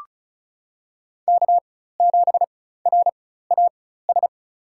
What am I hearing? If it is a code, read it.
Morse code, heard as a keyed tone, about 34 wpm, spaced as for 16 wpm, 700 Hz. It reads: K7RAS
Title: USA Call Signs - 5 Characters - 4x Character Spacing - 35wpm